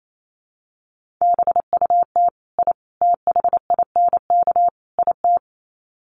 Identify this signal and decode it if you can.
Morse code, heard as a keyed tone, about 28 wpm, 700 Hz. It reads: BUT I THINK IT
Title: Sets of 4 Words 28wpm